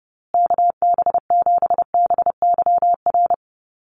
Morse code, heard as a keyed tone, about 30 wpm, 700 Hz. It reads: KB7BYR
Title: USA Call Signs Repeated 2x 30wpm